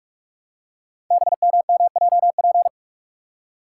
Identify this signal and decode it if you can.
Morse code, heard as a keyed tone, about 44 wpm, 690 Hz. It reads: BMMJP
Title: Code Groups - 5 Letter:Numbers - 45wpm